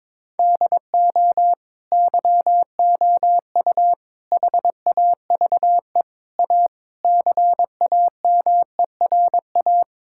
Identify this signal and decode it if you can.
Morse code, heard as a keyed tone, about 22 wpm, 700 Hz. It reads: DO YOU HAVE A CAMERA
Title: Sentences from Top 1000 Words - Review Part 5 - 22wpm